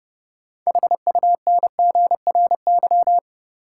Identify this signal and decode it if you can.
Morse code, heard as a keyed tone, about 30 wpm, 700 Hz. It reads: HUNGRY
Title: Sentences from Top 1400 Words - New Word Drill - 30wpm